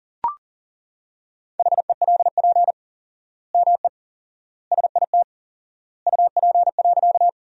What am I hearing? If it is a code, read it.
Morse code, heard as a keyed tone, about 37 wpm, 700 Hz. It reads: HELP ME SIT UP.
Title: 4x Word-Spacing - Sentences from Top 500 Words - 40wpm